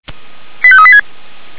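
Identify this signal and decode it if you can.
Five-tone selective call (ZVEI), heard as quick runs of five tones, tones of about 70 ms. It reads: 85386
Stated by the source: ZVEI five-tone selective call received by SDR.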